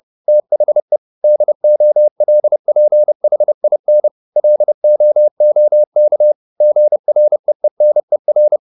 Morse code, heard as a keyed tone, about 30 wpm, 600 Hz. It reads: THE DOLPHIN LOOK GREENER
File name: the old man and the sea 2 30wpm